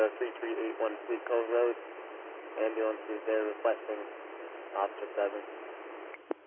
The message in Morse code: none

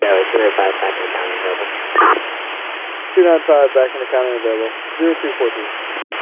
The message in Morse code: none

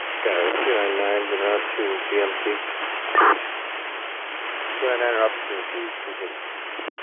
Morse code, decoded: none

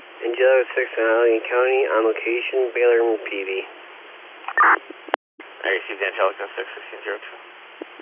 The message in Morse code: none

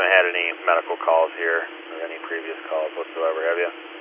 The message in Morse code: none

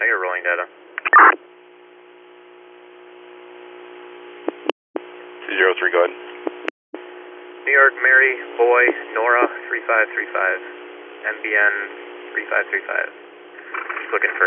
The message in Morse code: none

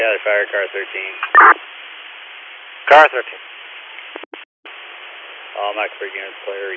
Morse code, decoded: none

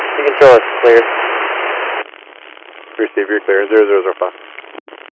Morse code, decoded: none

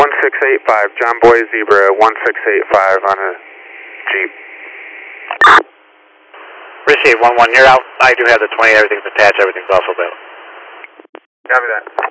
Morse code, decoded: none

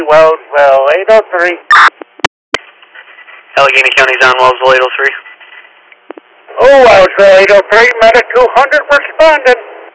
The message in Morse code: none

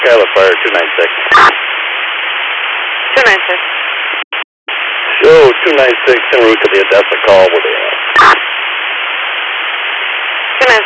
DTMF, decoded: none

